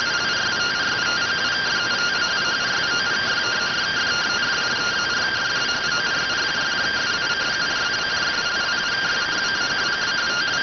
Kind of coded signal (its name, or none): two-tone teleprinter signal